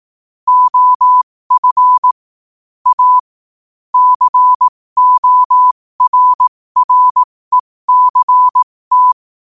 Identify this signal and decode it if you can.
Morse code, heard as a keyed tone, about 18 wpm, spaced as for 15 wpm, 1000 Hz. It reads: OF A CORRECT
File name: qotdfiles / 15wpm